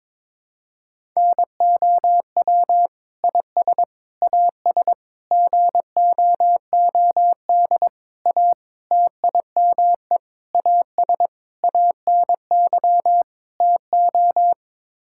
Morse code, heard as a keyed tone, about 22 wpm, 700 Hz. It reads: NOW IS AS GOOD A TIME AS ANY TO
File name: Sets of 9 Words - 22wpm